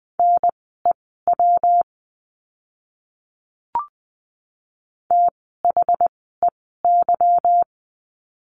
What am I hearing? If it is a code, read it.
Morse code, heard as a keyed tone, about 20 wpm, spaced as for 14 wpm, 700 Hz. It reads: NEW THEY
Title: 244 Lesson 39f - Letter z - Words 20wpm